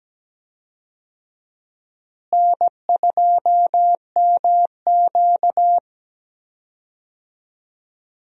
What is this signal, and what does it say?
Morse code, heard as a keyed tone, about 17 wpm, 700 Hz. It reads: N2MQ
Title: USA Call Signs - 4 Characters - Repeated 3x - 17wpm